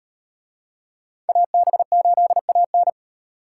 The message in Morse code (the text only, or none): AB8AN